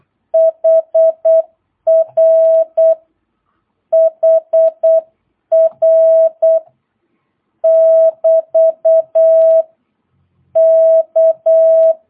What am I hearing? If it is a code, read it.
Morse code, heard as a keyed tone, about 8 wpm, 650 Hz. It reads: HR HR <BT> K
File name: jh8cbh / 2022cw